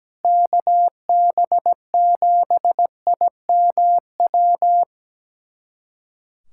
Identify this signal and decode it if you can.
Morse code, heard as a keyed tone, about 17 wpm, 700 Hz. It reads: KB7IMW
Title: USA Callsign DE Callsign 17wpm